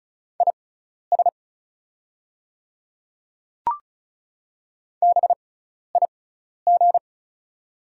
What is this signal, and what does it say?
Morse code, heard as a keyed tone, about 36 wpm, spaced as for 12 wpm, 700 Hz. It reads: IS BIG
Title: 6x Character Spacing - Top 200 Words - 35wpm